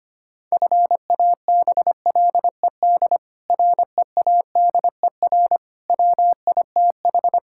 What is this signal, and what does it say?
Morse code, heard as a keyed tone, about 25 wpm, 700 Hz. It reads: FABLED READER WITH
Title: Pangram Challenge v2 - 25wpm